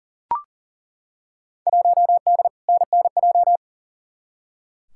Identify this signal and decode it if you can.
Morse code, heard as a keyed tone, about 40 wpm, 700 Hz. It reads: 1D NNJ
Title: Field Day Exchanges Encore - 40wpm